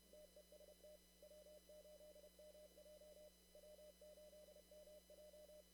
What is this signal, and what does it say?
Morse code, heard as a keyed tone, about 31 wpm, 600 Hz, with noise. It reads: TEST W8MJ W8MJ
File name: CW / kt5j wpx cw-op-k5pi